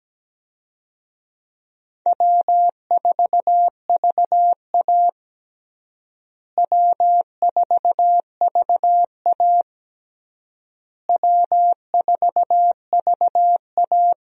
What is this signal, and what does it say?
Morse code, heard as a keyed tone, about 17 wpm, 700 Hz. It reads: W4VA W4VA W4VA
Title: USA Call Signs - 4 Characters - Repeated 3x - 17wpm